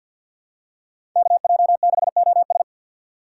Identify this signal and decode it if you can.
Morse code, heard as a keyed tone, about 47 wpm, 700 Hz. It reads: KJ6OS
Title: USA Call Signs - 5 Characters - 6x Character Spacing - 50wpm